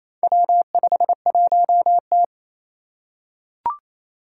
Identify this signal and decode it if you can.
Morse code, heard as a keyed tone, about 27 wpm, 700 Hz. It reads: W51T
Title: Four Challenging Letter-Number Combinations 28wpm